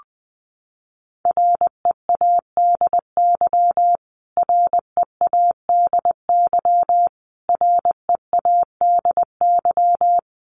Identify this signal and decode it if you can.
Morse code, heard as a keyed tone, about 19 wpm, 700 Hz. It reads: READY READY READY
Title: Top 500 Words Repeated 20wpm